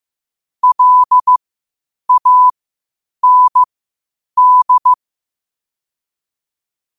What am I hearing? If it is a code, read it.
Morse code, heard as a keyed tone, about 15 wpm, spaced as for 8 wpm, 1000 Hz. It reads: LAND
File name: qotdfiles / 8wpm